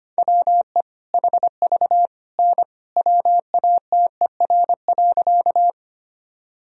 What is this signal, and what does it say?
Morse code, heard as a keyed tone, about 25 wpm, 700 Hz. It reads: WE HV N WATER.
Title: Sentences from Top 100 Words and Abbreviations 25wpm